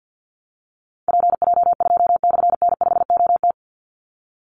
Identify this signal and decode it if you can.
Morse code, heard as a keyed tone, about 50 wpm, 700 Hz. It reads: PJ2/N5OT
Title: Call Signs - 50wpm